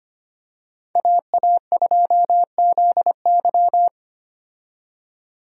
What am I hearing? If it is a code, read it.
Morse code, heard as a keyed tone, about 25 wpm, 700 Hz. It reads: AA2ZY
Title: USA Call Signs - 5 Characters - 25wpm